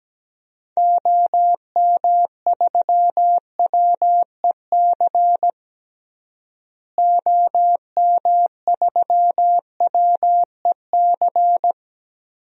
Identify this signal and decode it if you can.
Morse code, heard as a keyed tone, about 17 wpm, 700 Hz. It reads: OM3WEC OM3WEC
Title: DXCC-100 Call Signs - 6 Characters - Repeated 2x - 17wpm